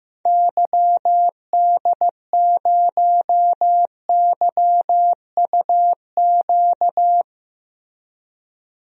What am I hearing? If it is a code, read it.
Morse code, heard as a keyed tone, about 15 wpm, 700 Hz. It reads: YD0YUQ
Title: DXCC-100 Call Signs - 6 Characters - Repeated 2x - 15wpm